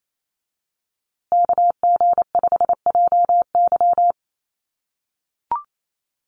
Morse code, heard as a keyed tone, about 28 wpm, 700 Hz. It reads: KG5JY
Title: USA Call Signs Repeated 3x 28wpm